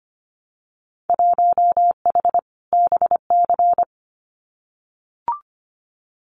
Morse code, heard as a keyed tone, about 25 wpm, 700 Hz. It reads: 1H BC
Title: Winter Field Day Exchanges Encore 25wpm